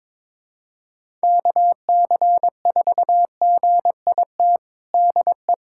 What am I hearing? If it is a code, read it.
Morse code, heard as a keyed tone, about 22 wpm, 700 Hz. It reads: KC4GIT DE
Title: USA Callsign DE Callsign 22wpm